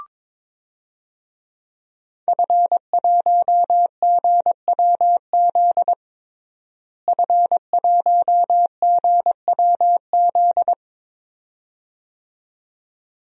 Morse code, heard as a keyed tone, about 21 wpm, 700 Hz. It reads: F1GWZ F1GWZ
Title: DXCC-100 Call Signs - 5 Characters - Repeated 2x - 22wpm